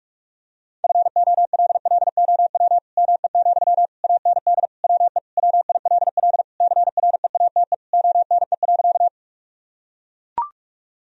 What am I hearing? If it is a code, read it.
Morse code, heard as a keyed tone, about 44 wpm, 700 Hz. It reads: FOLLOW ME, AND WE WILL CREATE ONE.
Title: Sentences from Top 1000 Words - Review Part 3 - 45wpm